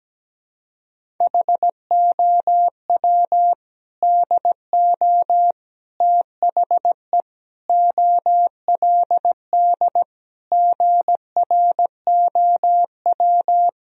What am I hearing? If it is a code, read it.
Morse code, heard as a keyed tone, about 17 wpm, 700 Hz. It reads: HOW DO THE OLD GROW
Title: Sentences from Top 1000 Words - Review Part 1 - 17wpm